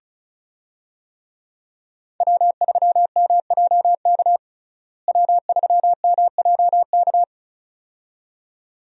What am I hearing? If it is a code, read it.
Morse code, heard as a keyed tone, about 36 wpm, 700 Hz. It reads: W3MJK W3MJK
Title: USA Call Signs - 5 Characters - Repeated 2x - 35wpm